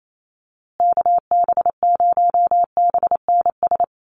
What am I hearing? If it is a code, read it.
Morse code, heard as a keyed tone, about 28 wpm, 700 Hz. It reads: KB0BNS
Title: USA Call Signs Repeated 2x 28wpm